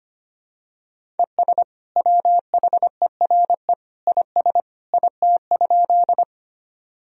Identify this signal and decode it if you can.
Morse code, heard as a keyed tone, about 25 wpm, 700 Hz. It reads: ES WHERE IS IT?
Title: Sentences from Top 200 Words and Abbreviations 25wpm